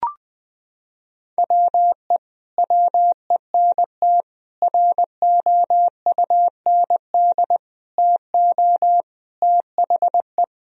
Morse code, heard as a keyed tone, about 20 wpm, 700 Hz. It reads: WE WENT ROUND TO THE
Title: Sentences from Top 1000 Words - Review Part 1 - 20wpm